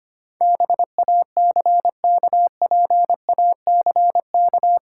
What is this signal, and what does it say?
Morse code, heard as a keyed tone, about 25 wpm, 700 Hz. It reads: BACKPACK
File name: practice / 25wpm / Easy Words with Hints - 25wpm